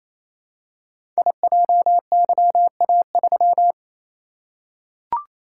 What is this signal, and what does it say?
Morse code, heard as a keyed tone, about 27 wpm, 690 Hz. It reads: IJYA3
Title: Code Groups - 5 Letter:Numbers - 28wpm